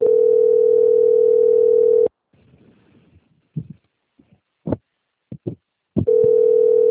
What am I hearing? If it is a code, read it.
Ringback tone (North America), 440 + 480 Hz.
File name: out- - - - - .31332